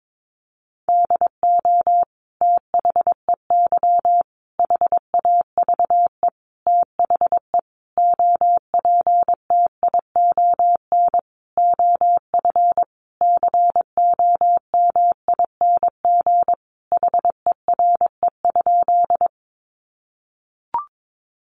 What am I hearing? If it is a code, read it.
Morse code, heard as a keyed tone, about 22 wpm, 700 Hz. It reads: DO THEY HAVE THE OPTION OF COMING HERE?
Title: Sentences from Top 1300 Words 22wpm